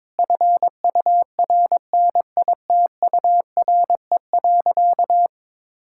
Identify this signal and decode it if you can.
Morse code, heard as a keyed tone, about 22 wpm, 690 Hz. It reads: FURNITURE.
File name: practice / 22wpm / Easy Sentences with Hints - 22wpm